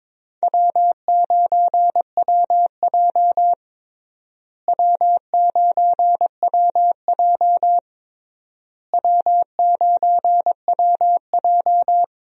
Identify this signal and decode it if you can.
Morse code, heard as a keyed tone, about 21 wpm, 700 Hz. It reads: W9WJ W9WJ W9WJ
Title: USA Call Signs - 4 Characters - Repeated 3x - 22wpm